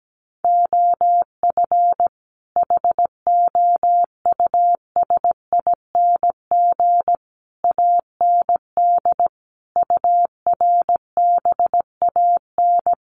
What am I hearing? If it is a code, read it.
Morse code, heard as a keyed tone, about 17 wpm, 700 Hz. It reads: OF HOUSING AND URBAN
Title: Sets of 5 Words Encore2 17wpm